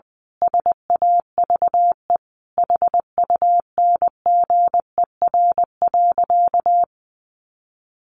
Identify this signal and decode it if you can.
Morse code, heard as a keyed tone, about 20 wpm, 700 Hz. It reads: SAVE HUNGER.
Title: A Sound Of Thunder 20wpm